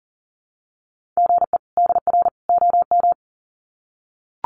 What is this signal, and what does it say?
Morse code, heard as a keyed tone, about 40 wpm, 700 Hz. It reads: GE DR OM
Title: Common QSO Phrases 40wpm